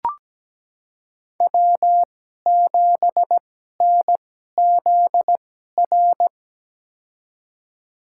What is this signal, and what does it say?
Morse code, heard as a keyed tone, about 17 wpm, spaced as for 12 wpm, 700 Hz. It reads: W7NZR
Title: USA Call Signs - 5 Characters - 2x Character Spacing - 17wpm